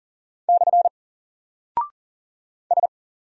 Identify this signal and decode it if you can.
Morse code, heard as a keyed tone, about 40 wpm, 700 Hz. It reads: / S
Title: Single Character (Letter:Number:Punctuation:Prosign) Alt 40wpm